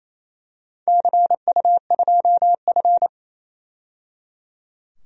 Morse code, heard as a keyed tone, about 28 wpm, 700 Hz. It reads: CU2F
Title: DXCC-100 Call Signs - 4 Characters - 28wpm